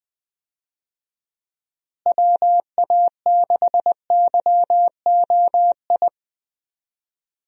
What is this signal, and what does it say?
Morse code, heard as a keyed tone, about 20 wpm, 700 Hz. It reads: WA6YOI
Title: USA Call Signs - 6 Characters - Repeated 3x - 20wpm